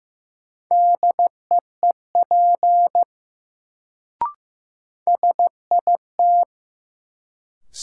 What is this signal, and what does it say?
Morse code, heard as a keyed tone, about 15 wpm, 700 Hz. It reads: DEEP SIT
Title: Sentences from Top 500 Words - New Word Drill - 15wpm